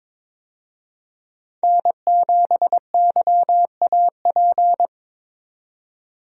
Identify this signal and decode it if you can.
Morse code, heard as a keyed tone, about 22 wpm, 700 Hz. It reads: N7YAP
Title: USA Call Signs - 5 Characters - 22wpm